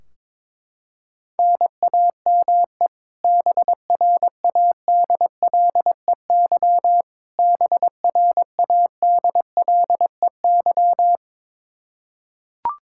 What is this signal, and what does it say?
Morse code, heard as a keyed tone, about 22 wpm, 700 Hz. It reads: NAME BRADLEY BRADLEY
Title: Practice QSO - Exchange 1 - Beginning - 22wpm